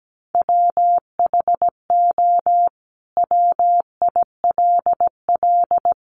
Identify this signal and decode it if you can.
Morse code, heard as a keyed tone, about 17 wpm, 700 Hz. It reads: WHO WILL
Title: Sentences from Top 100 Words 17wpm